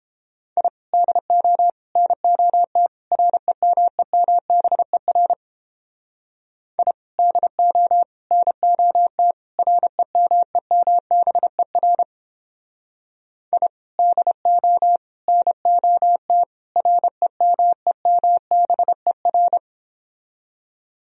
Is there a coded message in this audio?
Yes: Morse code from 0.6 to 19.6 s, heard as a keyed tone, about 30 wpm, 700 Hz. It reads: I DO NOT REMEMBER I DO NOT REMEMBER I DO NOT REMEMBER